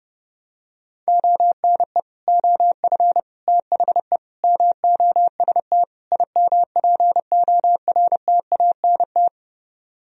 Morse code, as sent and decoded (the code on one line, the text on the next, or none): --- -. / --- ..-. / - .... / -- --- ... - / .. -- .--. --- .-. - .- -. -
ONE OF THE MOST IMPORTANT